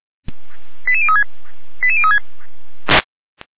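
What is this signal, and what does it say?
Five-tone selective call (ZVEI), heard as quick runs of five tones, tones of about 70 ms. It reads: 88036 88036
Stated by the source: ZVEI five-tone selective call received on an SDR, logged by the feuerwehr-orlamuende (Orlamünde fire brigade) server.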